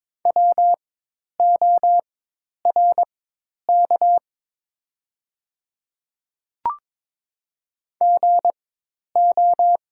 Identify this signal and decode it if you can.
Morse code, heard as a keyed tone, about 22 wpm, spaced as for 10 wpm, 700 Hz. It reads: WORK GO
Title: 4x Character Spacing - Top 200 Words - 22wpm